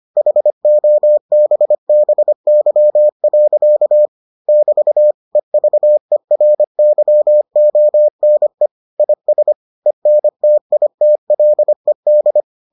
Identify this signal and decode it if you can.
Morse code, heard as a keyed tone, about 25 wpm, 600 Hz. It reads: HOBBY. <BT> EVERYONE IS ENTITLED